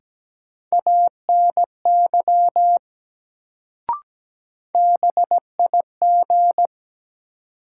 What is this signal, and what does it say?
Morse code, heard as a keyed tone, about 17 wpm, 700 Hz. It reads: ANY BIG
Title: Sentences from Top 200 Words - New Word Drill - 17wpm